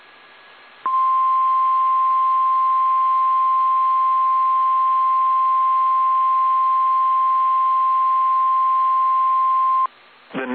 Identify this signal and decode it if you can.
Emergency Alert System (SAME) tones (the 1050 Hz attention tone), with speech. It is heard from 0.8 to 9.9 s.
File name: eas / KSDZKDJL / KSDZKDJL